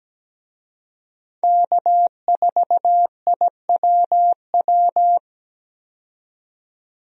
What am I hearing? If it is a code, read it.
Morse code, heard as a keyed tone, about 17 wpm, 700 Hz. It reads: K4IWW
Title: USA Call Signs - 5 Characters - 17wpm